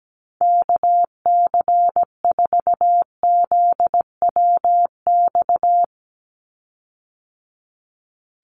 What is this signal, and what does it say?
Morse code, heard as a keyed tone, about 17 wpm, 700 Hz. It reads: KC4ZWX
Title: USA Call Signs 17wpm